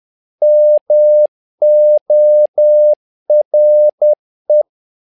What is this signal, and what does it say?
Morse code, heard as a keyed tone, about 10 wpm, 600 Hz. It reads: MORE
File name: learning center / tower trouble / tower trouble 10wpm